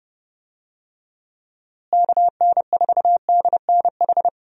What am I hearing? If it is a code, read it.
Morse code, heard as a keyed tone, about 30 wpm, 700 Hz. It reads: KN4DNH
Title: USA Call Signs - 6 Characters - Repeated 3x - 30wpm